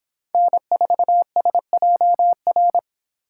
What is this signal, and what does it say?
Morse code, heard as a keyed tone, about 26 wpm, 700 Hz. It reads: N4SJR